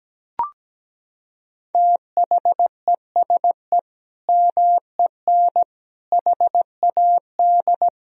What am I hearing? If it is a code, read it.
Morse code, heard as a keyed tone, about 17 wpm, 700 Hz. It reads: THESE MEN HAD
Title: Sentences from Top 1000 Words - Review Part 1 - 17wpm